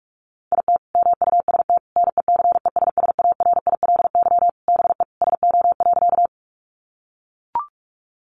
Morse code, heard as a keyed tone, about 44 wpm, 700 Hz. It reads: IT MUST NECESSARILY BE SO.
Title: Sentences from Top 1400 Words 45wpm